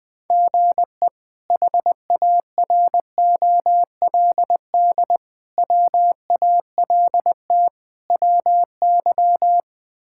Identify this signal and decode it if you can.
Morse code, heard as a keyed tone, about 20 wpm, 700 Hz. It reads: GE HAROLD WALT WY
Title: K1USN SST QSO Practice - 20wpm